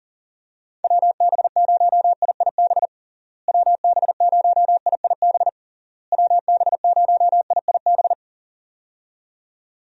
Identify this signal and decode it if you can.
Morse code, heard as a keyed tone, about 40 wpm, 700 Hz. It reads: WB0IIB WB0IIB WB0IIB
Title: USA Call Signs - 6 Characters - Repeated 3x - 40wpm